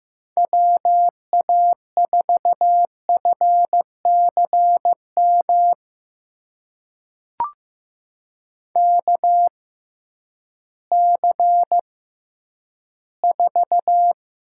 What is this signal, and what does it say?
Morse code, heard as a keyed tone, about 15 wpm, spaced as for 10 wpm, 700 Hz. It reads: WA4FCM KC4
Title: USA Call Signs - 6 Characters - 6x Character Spacing - 15wpm